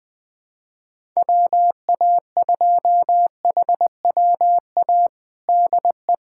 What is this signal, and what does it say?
Morse code, heard as a keyed tone, about 21 wpm, 700 Hz. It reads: WA2HWA DE
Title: USA Callsign DE Callsign - 6 Characters - 20wpm